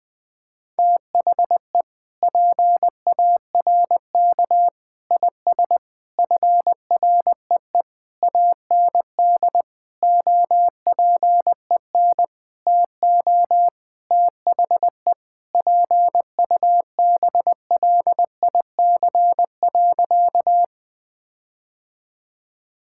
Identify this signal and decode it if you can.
Morse code, heard as a keyed tone, about 20 wpm, 700 Hz. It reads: THE PARK IS FREE AND OPEN TO THE PUBLIC.
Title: Sentences with 7-Word Phrases - Spoken Beginning - 20wpm